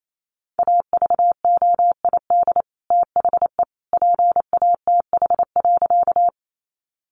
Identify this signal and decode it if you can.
Morse code, heard as a keyed tone, about 28 wpm, 700 Hz. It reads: AVOID THE PATH.
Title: Sentences from Top 1400 Words 28wpm